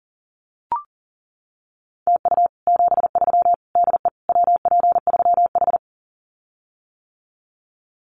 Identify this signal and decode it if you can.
Morse code, heard as a keyed tone, about 37 wpm, 700 Hz. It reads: TU 73 DE WP3H
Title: Common QSO Phrases Encore 40wpm